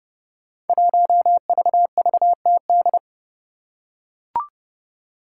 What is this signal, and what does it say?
Morse code, heard as a keyed tone, about 30 wpm, 700 Hz. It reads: 1VVTD
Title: Five Challenging Letter-Number Combinations 30wpm